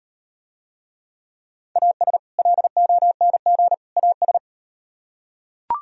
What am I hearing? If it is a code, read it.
Morse code, heard as a keyed tone, about 38 wpm, 700 Hz. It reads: AS LONG AS